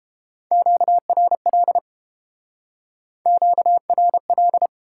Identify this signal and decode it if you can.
Morse code, heard as a keyed tone, about 31 wpm, 700 Hz. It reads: QRL QRL